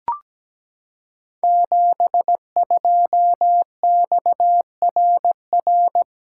Morse code, heard as a keyed tone, about 17 wpm, 700 Hz. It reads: 72XRR
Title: Code Groups - 5 Letter:Numbers - 17wpm